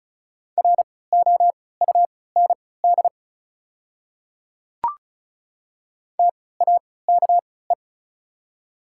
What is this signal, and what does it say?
Morse code, heard as a keyed tone, about 34 wpm, spaced as for 20 wpm, 690 Hz. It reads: ROUND TAKE
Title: 3x Character Spacing - Top 200 Words - 35wpm